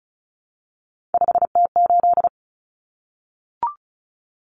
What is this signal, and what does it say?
Morse code, heard as a keyed tone, about 34 wpm, 700 Hz. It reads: 5T8
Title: Common Cut Numbers - 3-Digit 35wpm